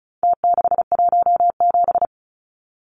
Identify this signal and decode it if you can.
Morse code, heard as a keyed tone, about 34 wpm, 700 Hz. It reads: T617
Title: Numbers - Contest Style - 4-Digit 35wpm